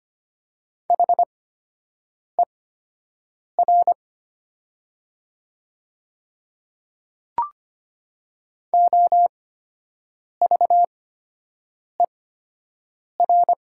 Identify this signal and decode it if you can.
Morse code, heard as a keyed tone, about 25 wpm, spaced as for 7 wpm, 700 Hz. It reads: HER OVER
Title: 8x Character Spacing - Top 100 Words - 25wpm